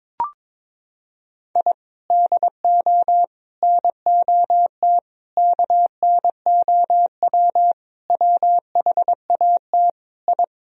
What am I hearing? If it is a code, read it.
Morse code, heard as a keyed tone, about 22 wpm, 700 Hz. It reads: I DO NOT KNOW WHAT I
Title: Sentences from Top 1000 Words - Review Part 3 - 22wpm